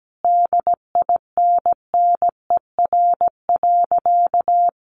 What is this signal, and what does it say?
Morse code, heard as a keyed tone, about 17 wpm, 700 Hz. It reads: DINNER.
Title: Easy Phrases 17wpm